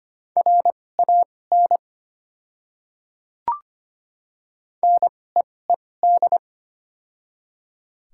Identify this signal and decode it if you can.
Morse code, heard as a keyed tone, about 25 wpm, spaced as for 18 wpm, 700 Hz. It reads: RAN NEED
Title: 2x Character Spacing - Top 500 Words - 25wpm